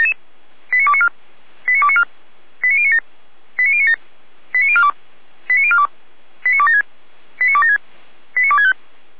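Five-tone selective call (ZVEI), heard as quick runs of five tones, tones of about 70 ms. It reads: 89294 89294 89097 89097 89042 89042 89276 89276 89265
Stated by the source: ZVEI five-tone selective call received by SDR, fire brigade feuerwehr-orlamuende, address 89140.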